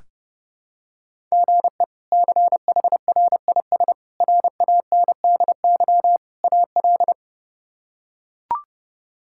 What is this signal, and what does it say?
Morse code, heard as a keyed tone, about 30 wpm, 700 Hz. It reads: GE CHRIS RANDY AL
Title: K1USN SST QSO Practice - 30wpm